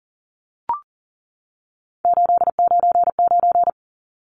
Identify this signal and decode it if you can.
Morse code, heard as a keyed tone, about 40 wpm, 700 Hz. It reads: 899
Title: Numbers - 3-Digit 40wpm